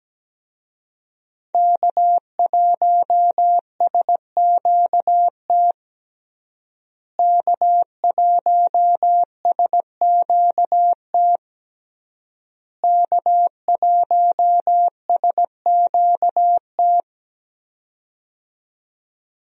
Morse code, heard as a keyed tone, about 17 wpm, 700 Hz. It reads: K1SQT K1SQT K1SQT
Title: USA Call Signs - 5 Characters - Repeated 3x - 17wpm